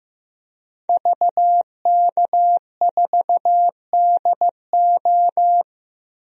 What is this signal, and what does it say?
Morse code, heard as a keyed tone, about 15 wpm, 700 Hz. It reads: VK4DO
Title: DXCC-100 Call Signs - 5 Characters - 15wpm